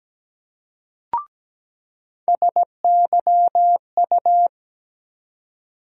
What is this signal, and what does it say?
Morse code, heard as a keyed tone, about 17 wpm, 700 Hz. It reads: SYU
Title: USA Call Sign Postfixes - 3 Letters 17wpm